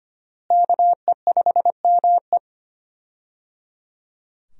Morse code, heard as a keyed tone, about 25 wpm, 700 Hz. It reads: KE5ME
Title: USA Call Signs - 5 Characters - Repeated 3x - 25wpm